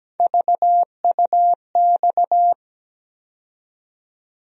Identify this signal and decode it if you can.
Morse code, heard as a keyed tone, about 17 wpm, 700 Hz. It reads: VUX
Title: USA Call Sign Postfixes - 3 Letters 17wpm